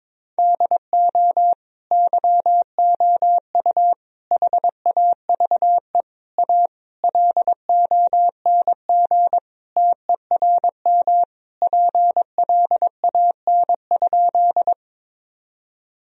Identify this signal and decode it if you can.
Morse code, heard as a keyed tone, about 22 wpm, 700 Hz. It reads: DO YOU HAVE A LONG TERM PLAN?
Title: Sentences from Top 1000 Words - Review Part 4 - 22wpm